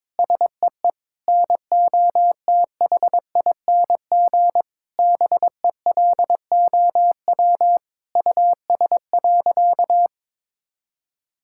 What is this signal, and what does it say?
Morse code, heard as a keyed tone, about 22 wpm, 700 Hz. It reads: SEE NOTHING BELOW US.
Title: Sentences from Top 1000 Words - Review Part 5 - 22wpm